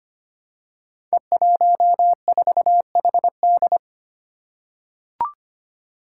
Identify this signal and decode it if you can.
Morse code, heard as a keyed tone, about 25 wpm, 700 Hz. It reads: E14HD
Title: Five Challenging Letter-Number Combinations 25wpm